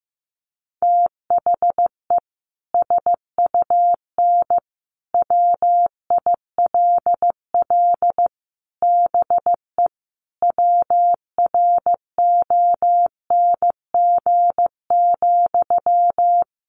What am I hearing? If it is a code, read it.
Morse code, heard as a keyed tone, about 15 wpm, 700 Hz. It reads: THE SUN WILL BE WRONG,
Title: Sentences from Top 900 Words 15wpm